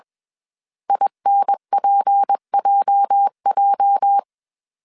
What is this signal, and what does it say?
Morse code, heard as a keyed tone, about 21 wpm, 800 Hz. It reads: INPJJ